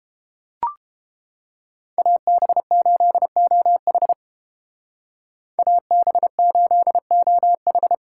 Morse code, heard as a keyed tone, about 31 wpm, 700 Hz. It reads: AB8OH AB8OH